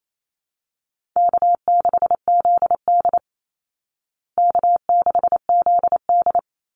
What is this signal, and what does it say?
Morse code, heard as a keyed tone, about 28 wpm, 700 Hz. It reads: K6ZD K6ZD
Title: Easy Call Signs Repeated 2x - 28wpm